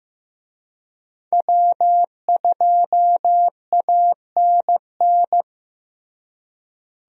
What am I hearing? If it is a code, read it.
Morse code, heard as a keyed tone, about 15 wpm, 700 Hz. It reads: W2ANN
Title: USA Call Signs - 5 Characters - Repeated 3x - 15wpm